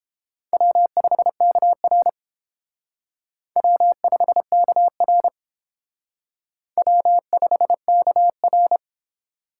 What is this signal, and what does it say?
Morse code, heard as a keyed tone, about 31 wpm, 700 Hz. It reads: W5KR W5KR W5KR